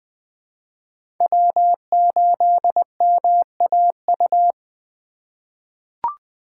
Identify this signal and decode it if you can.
Morse code, heard as a keyed tone, about 20 wpm, 700 Hz. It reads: W8MAU
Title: USA Call Signs - 5 Characters - 3x Character Spacing - 20wpm